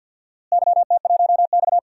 Morse code, heard as a keyed tone, about 50 wpm, 700 Hz. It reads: YT1X
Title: DXCC-100 Call Signs - 4 Characters - Repeated 2x - 50wpm